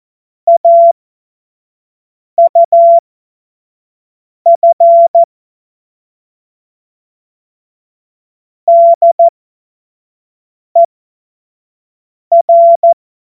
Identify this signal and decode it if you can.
Morse code, heard as a keyed tone, about 14 wpm, spaced as for 5 wpm, 680 Hz. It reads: AUF DER
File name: Morsekurs / Witze / Witz 10.Oktober 2023 25 F70